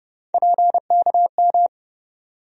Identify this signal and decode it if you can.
Morse code, heard as a keyed tone, about 32 wpm, 700 Hz. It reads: PKM
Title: USA Call Sign Postfixes - 3 Letters 30wpm